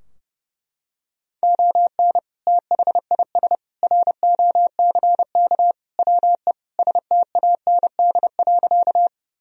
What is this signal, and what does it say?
Morse code, heard as a keyed tone, about 32 wpm, 700 Hz. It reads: ON THIS ROCK WE STAND.
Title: Sentences from Top 1000 Words - Review Part 2 - 30wpm